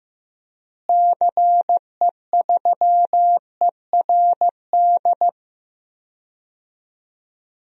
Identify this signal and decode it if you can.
Morse code, heard as a keyed tone, about 15 wpm, 700 Hz. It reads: CE3ERD
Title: DXCC-100 Call Signs - 6 Characters - Repeated 2x - 15wpm